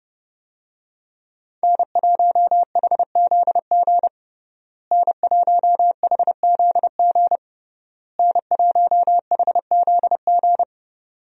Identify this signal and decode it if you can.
Morse code, heard as a keyed tone, about 30 wpm, 700 Hz. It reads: N1HZG N1HZG N1HZG
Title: USA Call Signs - 5 Characters - Repeated 3x - 30wpm